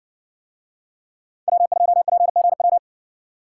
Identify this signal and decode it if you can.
Morse code, heard as a keyed tone, about 60 wpm, 700 Hz. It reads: W2PGW